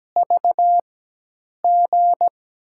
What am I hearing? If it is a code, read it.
Morse code, heard as a keyed tone, about 17 wpm, spaced as for 8 wpm, 700 Hz. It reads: VG
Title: USA Call Signs - 4x Character Spacing - 17wpm